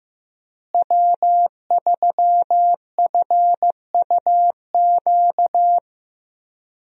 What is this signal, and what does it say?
Morse code, heard as a keyed tone, about 16 wpm, 700 Hz. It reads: W3FUQ
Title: USA Call Signs - 5 Characters - Repeated 2x - 15wpm